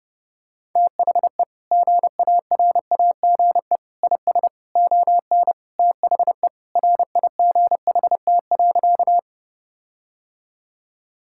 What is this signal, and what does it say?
Morse code, heard as a keyed tone, about 30 wpm, 700 Hz. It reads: THE GARAGE IS ON THE RIGHT.
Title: Sentences from Top 1600 Words 30wpm